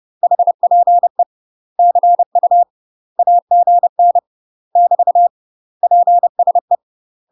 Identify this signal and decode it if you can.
Morse code, heard as a keyed tone, about 30 wpm, 700 Hz. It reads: HPE CU AGN <BT> PSE
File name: UK MASTER QSOs TRAINING/30WPM